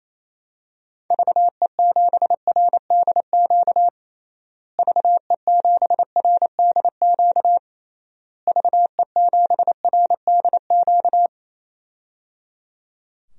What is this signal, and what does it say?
Morse code, heard as a keyed tone, about 28 wpm, 700 Hz. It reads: VE7RDQ VE7RDQ VE7RDQ
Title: DXCC-100 Call Signs - 6 Characters - Repeated 3x - 28wpm